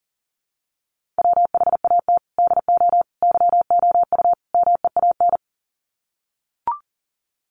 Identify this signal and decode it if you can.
Morse code, heard as a keyed tone, about 37 wpm, 700 Hz. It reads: WHAT DO YOU MEAN
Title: Sets of 4 Words 40wpm